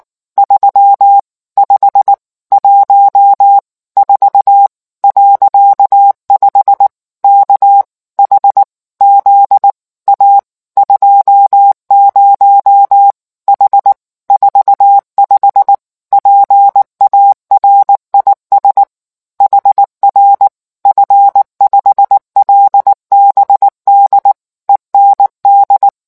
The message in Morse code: 3514.5 KHZ A 20 H 45 PARIS HR F5LBD END

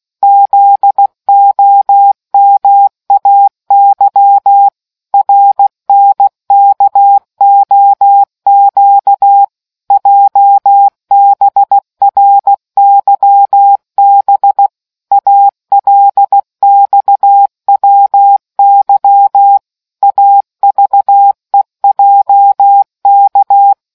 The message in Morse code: ZOMAY RNKOQ JBRYB ALXWY AVEJK

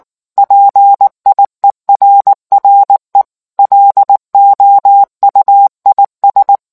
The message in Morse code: PIERRE LOUIS

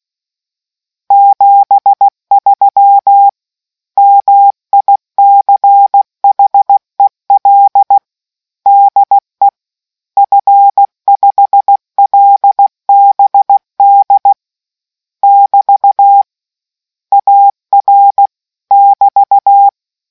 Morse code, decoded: 73 MICHEL DE F5LBD <BT> AR <BT>

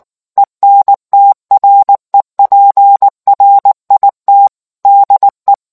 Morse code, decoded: ENTREPRIT DE

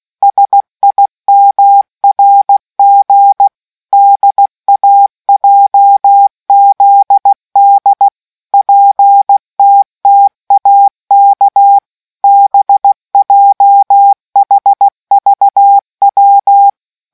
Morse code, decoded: SIMRG DAJZD PTTAK BJHVW

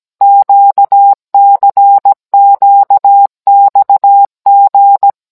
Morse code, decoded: QCQXG

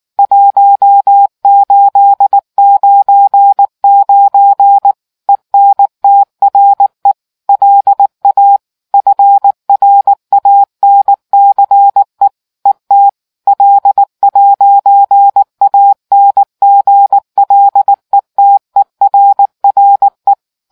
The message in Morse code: 1899 ENTRE LA FRANCE ET L'ANGLETERRE

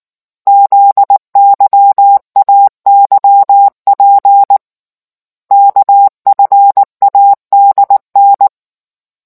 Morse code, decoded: ZYAYP KFADN

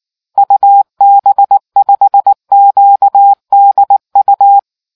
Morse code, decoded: UB5QDU

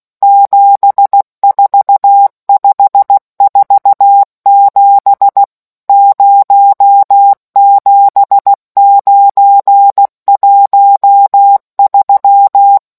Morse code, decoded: 74547 07913

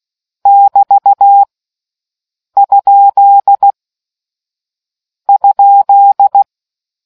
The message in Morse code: <BT> ? ?

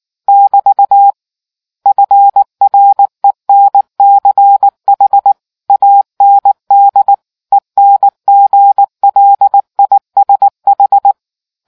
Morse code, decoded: <BT> FRENCH AND ENGLISH

